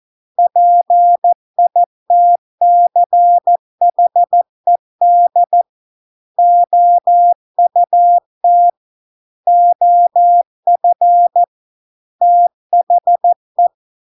Morse code, heard as a keyed tone, about 14 wpm, 700 Hz. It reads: PITCHED OUT OF THE